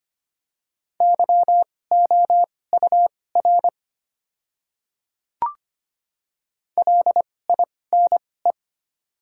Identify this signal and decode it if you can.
Morse code, heard as a keyed tone, about 25 wpm, spaced as for 18 wpm, 690 Hz. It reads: YOUR LINE
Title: 2x Character Spacing - Top 200 Words - 25wpm